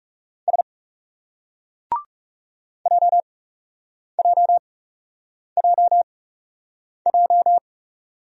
Morse code, heard as a keyed tone, about 38 wpm, spaced as for 8 wpm, 700 Hz. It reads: S JJJJ